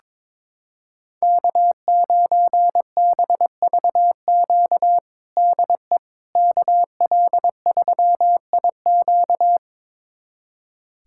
Morse code, heard as a keyed tone, about 22 wpm, 700 Hz. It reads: K9BVQ DE KL3IQ
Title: USA Callsign DE Callsign - 5 Characters - 22wpm